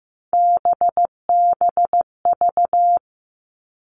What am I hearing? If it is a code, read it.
Morse code, heard as a keyed tone, about 15 wpm, 700 Hz. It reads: BBV
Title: All Cut Numbers - 3-Digit 15wpm